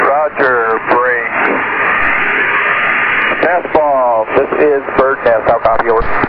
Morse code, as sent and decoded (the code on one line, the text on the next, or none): none